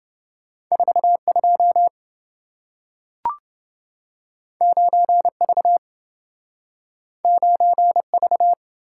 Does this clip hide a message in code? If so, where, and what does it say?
Yes: Morse code from 0.7 to 8.5 s, heard as a keyed tone, about 27 wpm, 700 Hz. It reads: 42 9V 9V